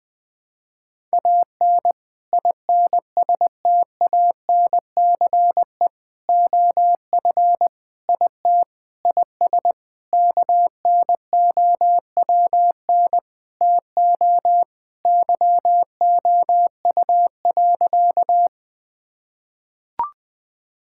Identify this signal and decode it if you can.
Morse code, heard as a keyed tone, about 21 wpm, 700 Hz. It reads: AN INSTANCE OF IT IS KNOWN TO YOU.
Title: Sentences from Top 1000 Words - Review Part 5 - 20wpm